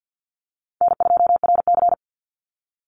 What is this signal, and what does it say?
Morse code, heard as a keyed tone, about 49 wpm, 700 Hz. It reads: N2RC
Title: Easy Call Signs Repeated 3x - 50wpm